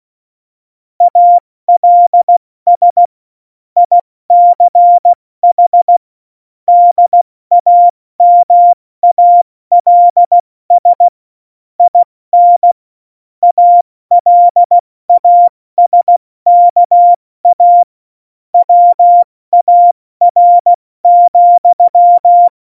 Morse code, heard as a keyed tone, about 16 wpm, 700 Hz. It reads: ALS ICH DAMALS IN ALASKA WAR,
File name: Morsekurs / Witze / Witz 12. September 17, Tempo 70